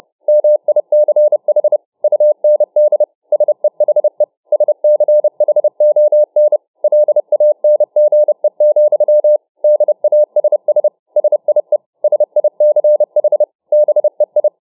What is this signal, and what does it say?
Morse code, heard as a keyed tone, about 29 wpm, 600 Hz. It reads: MICH UND SEHE SCHON LANGE, DASS SIE SICH BEI
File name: dk5ke / Morsetanz 30WpM